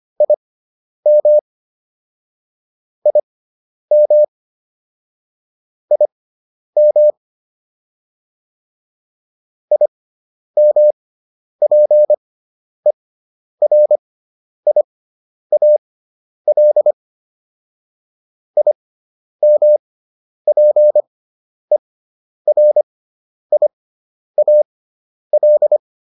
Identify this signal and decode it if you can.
Morse code, heard as a keyed tone, about 25 wpm, spaced as for 10 wpm, 600 Hz. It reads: IM IM IM IMPERIAL IMPERIAL